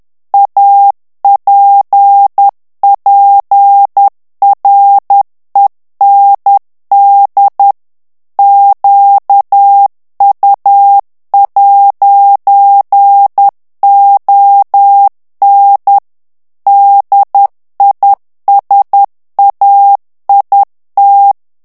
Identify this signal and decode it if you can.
Morse code, heard as a keyed tone, about 10 wpm, 800 Hz. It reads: APPREND QU'ON DISAIT
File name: morse mp3 / FRANCAIS TEXTE CLAIR 2 VITESSE 10 MOTS MINUTES